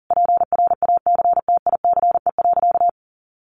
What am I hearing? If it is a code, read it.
Morse code, heard as a keyed tone, about 40 wpm, 700 Hz. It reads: PRACTICE.
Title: Sentences from Top 900 Words 40wpm